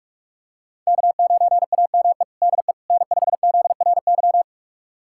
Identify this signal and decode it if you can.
Morse code, heard as a keyed tone, about 42 wpm, 700 Hz. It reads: K9AME DE N5ZRY
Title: USA Callsign DE Callsign - 5 Characters - 45wpm